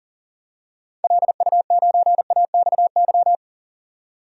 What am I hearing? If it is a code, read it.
Morse code, heard as a keyed tone, about 40 wpm, 700 Hz. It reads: LU9AXY
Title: DXCC-100 Call Signs - 6 Characters - 40wpm